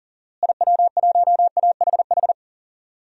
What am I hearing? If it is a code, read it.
Morse code, heard as a keyed tone, about 40 wpm, 700 Hz. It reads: IW1AHH
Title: DXCC-100 Call Signs - 6 Characters - Repeated 2x - 40wpm